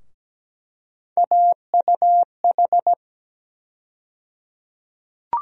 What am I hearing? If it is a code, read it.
Morse code, heard as a keyed tone, about 17 wpm, 700 Hz. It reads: AUH
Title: USA Call Sign Postfixes - 3 Letters 17wpm